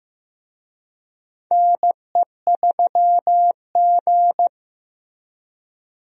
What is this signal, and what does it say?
Morse code, heard as a keyed tone, about 15 wpm, 700 Hz. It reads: NE3G
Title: USA Call Signs - 4 Characters - Repeated 3x - 15wpm